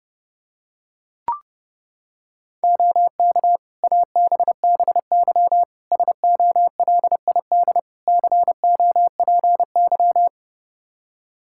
Morse code, heard as a keyed tone, about 30 wpm, 700 Hz. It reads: OK ABBY SOLID COPY
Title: Practice QSO - Exchange 3 - More Info - 30wpm